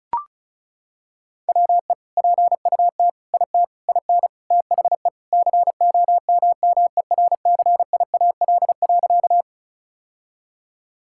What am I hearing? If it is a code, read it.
Morse code, heard as a keyed tone, about 36 wpm, 700 Hz. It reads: WE PUT IT IN THE COMMERCIAL.
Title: Sentences from Top 1000 Words - Review Part 5 - 35wpm